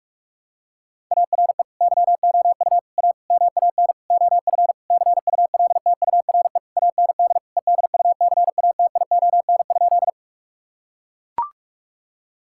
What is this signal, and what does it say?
Morse code, heard as a keyed tone, about 41 wpm, 700 Hz. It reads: ARE YOU A MAN OF CULTURE AND EDUCATION?
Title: Sentences from Top 1000 Words - Review Part 6 - 45wpm